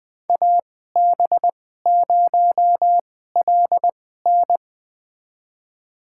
Morse code, heard as a keyed tone, about 20 wpm, spaced as for 15 wpm, 700 Hz. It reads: AB0LN
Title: USA Call Signs - 5 Characters - 2x Character Spacing - 20wpm